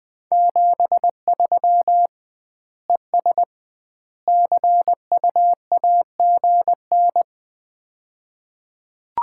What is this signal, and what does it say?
Morse code, heard as a keyed tone, about 20 wpm, 700 Hz. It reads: 73 ES CUAGN
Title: Common QSO Phrases - 2x Word-Spacing - 20wpm